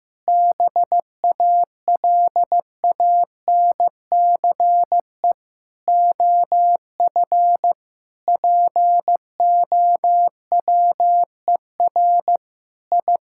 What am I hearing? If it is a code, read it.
Morse code, heard as a keyed tone, about 16 wpm, 700 Hz. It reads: BALANCE OF POWER I
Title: Sentences from Top 1000 Words - Review Part 6 - 15wpm